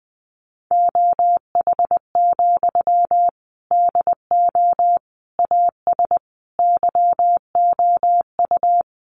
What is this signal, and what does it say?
Morse code, heard as a keyed tone, about 20 wpm, 700 Hz. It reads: OH, DO AS YOU
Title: Sentences from Top 500 Words 20wpm